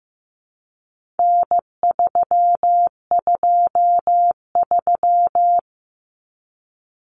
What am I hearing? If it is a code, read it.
Morse code, heard as a keyed tone, about 15 wpm, 700 Hz. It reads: N323
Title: Common Cut Numbers - 4-Digit 15wpm